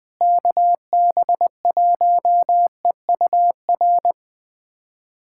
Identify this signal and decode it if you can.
Morse code, heard as a keyed tone, about 20 wpm, 700 Hz. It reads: KB1EUR
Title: USA Callsign DE Callsign - 6 Characters - 20wpm